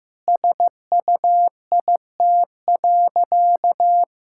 Morse code, heard as a keyed tone, about 15 wpm, 700 Hz. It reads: SUIT.
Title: Sentences from Top 1600 Words 15wpm